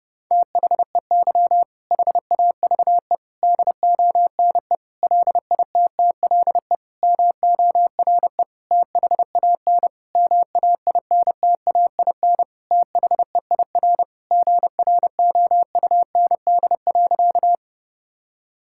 Morse code, heard as a keyed tone, about 32 wpm, 700 Hz. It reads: THEY HAVE DONE LITTLE MORE THAN MAINTAIN THEIR GROUND.
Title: Sentences from Top 1000 Words - Review Part 5 - 30wpm